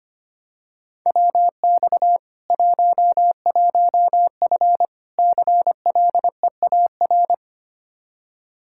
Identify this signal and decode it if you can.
Morse code, heard as a keyed tone, about 25 wpm, 700 Hz. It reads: WX 11F CLEAR
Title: Practice QSO - Exchange 2 - Info - 25wpm